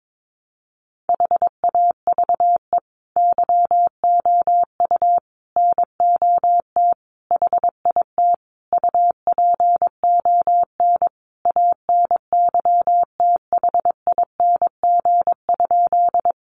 Morse code, heard as a keyed tone, about 22 wpm, 700 Hz. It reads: HAVE YOU NOT HIT UPON ANYTHING?
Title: Sentences from Top 1200 Words 22wpm